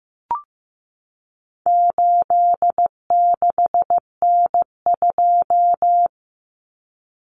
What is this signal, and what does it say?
Morse code, heard as a keyed tone, about 15 wpm, 700 Hz. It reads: 86N2
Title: Common Cut Numbers - 4-Digit 15wpm